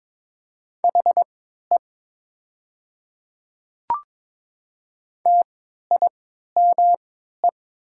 Morse code, heard as a keyed tone, about 23 wpm, spaced as for 13 wpm, 690 Hz. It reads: HE TIME
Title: 3x Character Spacing - Top 300 Words - 22wpm